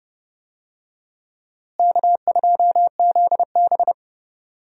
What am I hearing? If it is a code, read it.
Morse code, heard as a keyed tone, about 31 wpm, 700 Hz. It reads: K2ZB
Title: USA Call Signs - 4 Characters - Repeated 3x - 30wpm